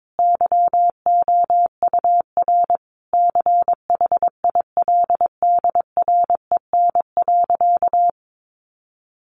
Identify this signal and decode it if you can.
Morse code, heard as a keyed tone, about 22 wpm, 700 Hz. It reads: YOUR CHILDREN.
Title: Sentences from Top 1400 Words 22wpm